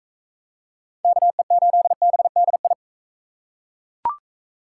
Morse code, heard as a keyed tone, about 39 wpm, 700 Hz. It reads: KE8BDI